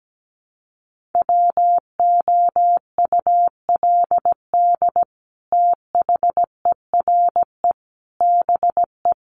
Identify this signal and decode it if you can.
Morse code, heard as a keyed tone, about 17 wpm, 700 Hz. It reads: WOULD THERE BE
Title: Sentences from Top 1500 Words 17wpm